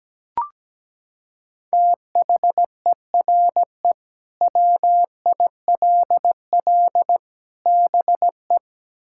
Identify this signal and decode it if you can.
Morse code, heard as a keyed tone, about 17 wpm, 700 Hz. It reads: THERE WILL BE
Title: Sentences from Top 1600 Words 17wpm